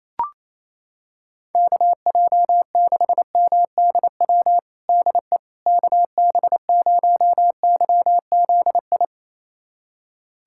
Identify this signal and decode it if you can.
Morse code, heard as a keyed tone, about 28 wpm, 700 Hz. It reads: KJ6MDW DE KB0YZI
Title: USA Callsign DE Callsign 28wpm